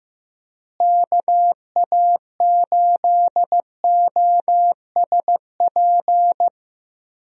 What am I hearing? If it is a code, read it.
Morse code, heard as a keyed tone, about 15 wpm, 700 Hz. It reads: KA8OSP
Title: USA Call Signs - 6 Characters - Repeated 2x - 15wpm